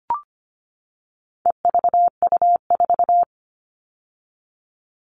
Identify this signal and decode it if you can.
Morse code, heard as a keyed tone, about 25 wpm, 700 Hz. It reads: EVU4
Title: All Cut Numbers - 4-Digit 25wpm